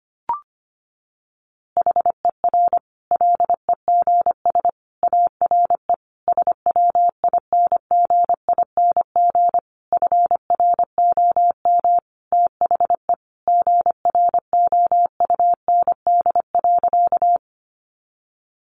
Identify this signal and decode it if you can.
Morse code, heard as a keyed tone, about 25 wpm, 700 Hz. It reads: HER LEGS ARE SWINGING FROM THE GROUND.
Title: Easy Phrases 25wpm